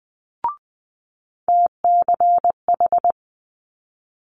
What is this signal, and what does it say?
Morse code, heard as a keyed tone, about 19 wpm, 700 Hz. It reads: TCH
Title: Multi-letter Phonemes Quick Fire 20wpm